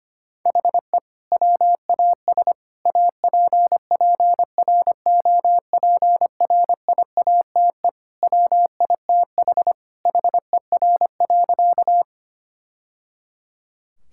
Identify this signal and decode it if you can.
Morse code, heard as a keyed tone, about 25 wpm, 700 Hz. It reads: HE WAS APPROPRIATE WITH HER.
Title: Sentences from Top 1000 Words - Review Part 5 - 25wpm